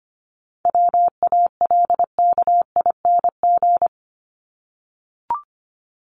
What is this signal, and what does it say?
Morse code, heard as a keyed tone, about 25 wpm, 700 Hz. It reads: WALKING
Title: ING Suffix - Top 100 Words with Base Words 25wpm